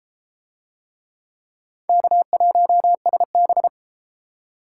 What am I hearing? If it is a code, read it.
Morse code, heard as a keyed tone, about 28 wpm, 700 Hz. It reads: K1SB